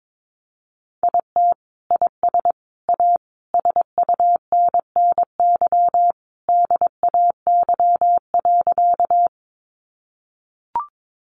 Morse code, heard as a keyed tone, about 22 wpm, 700 Hz. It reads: IT IS A SUNNY DAY.
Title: Easy Phrases 22wpm